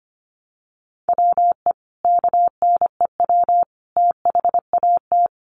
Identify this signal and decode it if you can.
Morse code, heard as a keyed tone, about 25 wpm, 700 Hz. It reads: WE KNEW THAT
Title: Sentences from Top 400 Words 25wpm